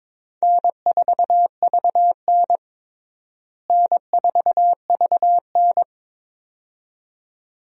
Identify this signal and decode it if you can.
Morse code, heard as a keyed tone, about 22 wpm, 700 Hz. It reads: N4VN N4VN
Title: USA Call Signs - 4 Characters - Repeated 2x - 22wpm